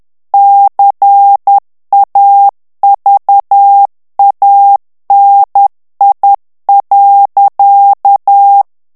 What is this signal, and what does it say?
Morse code, heard as a keyed tone, about 10 wpm, 800 Hz. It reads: CAVANI.
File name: morse mp3 / ITALIEN texte clair 1 vitesse 10 mots minute